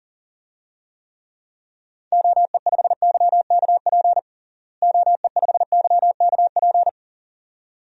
Morse code, heard as a keyed tone, about 40 wpm, 700 Hz. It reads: OE5YKP OE5YKP
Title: DXCC-100 Call Signs - 6 Characters - Repeated 2x - 40wpm